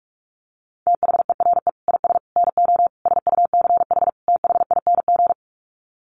Morse code, heard as a keyed tone, about 44 wpm, 700 Hz. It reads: THERE IS NO SUCH THING
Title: Sets of 5 Words Encore 45wpm